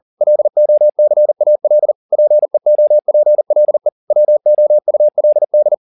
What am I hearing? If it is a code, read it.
Morse code, heard as a keyed tone, about 40 wpm, 600 Hz. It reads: LOCAL PEOPLE WOULD